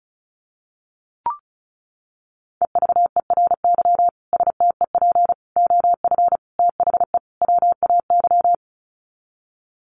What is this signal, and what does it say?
Morse code, heard as a keyed tone, about 34 wpm, 700 Hz. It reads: EVERY STEP OF THE WAY
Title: Sets of 5 Words Encore2 35wpm